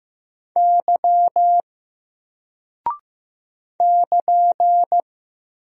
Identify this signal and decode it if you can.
Morse code, heard as a keyed tone, about 15 wpm, 700 Hz. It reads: Y <KN>
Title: Single Character (Letter:Number:Punctuation:Prosign) 15wpm